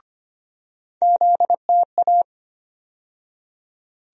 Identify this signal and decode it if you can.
Morse code, heard as a keyed tone, about 25 wpm, 700 Hz. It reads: ZTA
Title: USA Call Sign Postfixes - 3 Letters 25wpm